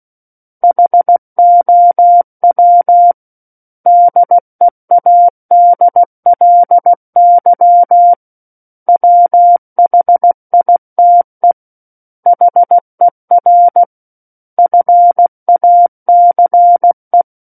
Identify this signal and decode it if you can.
Morse code, heard as a keyed tone, about 16 wpm, 700 Hz. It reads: HOW DEADLY WHITE HER FACE
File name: Book One / 07-Chapter 7 - 16 WPM